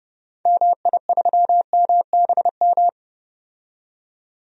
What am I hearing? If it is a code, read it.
Morse code, heard as a keyed tone, about 30 wpm, 700 Hz. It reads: MI3MBM
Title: DXCC-100 Call Signs - 6 Characters - 30wpm